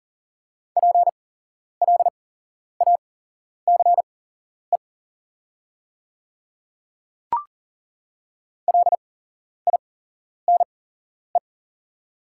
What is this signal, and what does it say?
Morse code, heard as a keyed tone, about 41 wpm, spaced as for 11 wpm, 690 Hz. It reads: PLACE LINE
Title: 8x Character Spacing - Top 300 Words - 40wpm